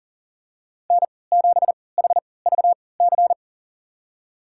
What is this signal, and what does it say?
Morse code, heard as a keyed tone, about 40 wpm, spaced as for 23 wpm, 700 Hz. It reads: N7HVC
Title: USA Call Signs - 5 Characters - 3x Character Spacing - 40wpm